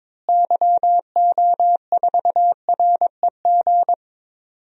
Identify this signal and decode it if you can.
Morse code, heard as a keyed tone, about 21 wpm, 700 Hz. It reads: YO4REG
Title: DXCC-100 Call Signs - 6 Characters - Repeated 3x - 22wpm